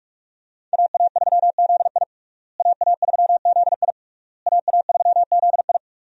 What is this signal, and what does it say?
Morse code, heard as a keyed tone, about 41 wpm, 700 Hz. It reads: AA3ZI AA3ZI AA3ZI
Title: USA Call Signs - 5 Characters - Repeated 3x - 45wpm